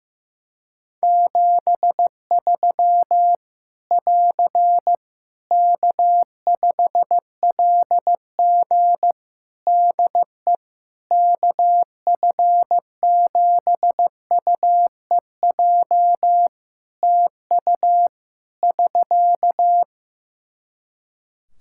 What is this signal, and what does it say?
Morse code, heard as a keyed tone, about 15 wpm, 700 Hz. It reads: 73 <AR> K5LG DE KF7UEJ TU <SK>
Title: K1USN SST QSO Practice 15wpm